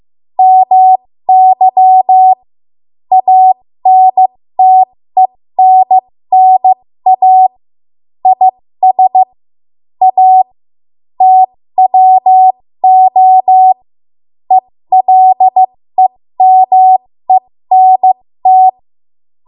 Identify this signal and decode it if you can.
Morse code, heard as a keyed tone, about 15 wpm, spaced as for 13 wpm, 750 Hz. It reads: MY ANTENNA IS A TWO ELEMENT
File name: Morse / 13WPM